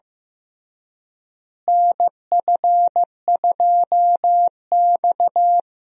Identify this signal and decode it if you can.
Morse code, heard as a keyed tone, about 15 wpm, 700 Hz. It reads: NF2X
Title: USA Call Signs - 4 Characters - Repeated 2x - 15wpm